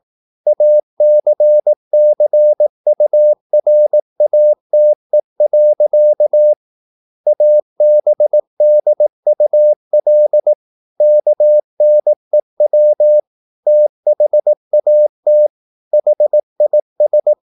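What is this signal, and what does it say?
Morse code, heard as a keyed tone, about 18 wpm, 600 Hz. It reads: ACCURATE. ABDUL KNEW THAT HIS
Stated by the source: Morse practice at 18 wpm, English text, from CW med Gnister.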